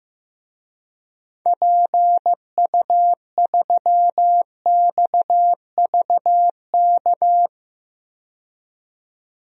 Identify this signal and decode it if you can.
Morse code, heard as a keyed tone, about 15 wpm, 700 Hz. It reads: PU3XVK
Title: DXCC-100 Call Signs - 6 Characters - 15wpm